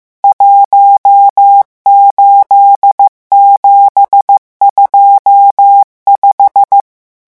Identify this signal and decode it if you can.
Morse code, heard as a keyed tone, about 15 wpm, 800 Hz. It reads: 18725
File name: morse mp3 / revision11 chiffres vitesse 14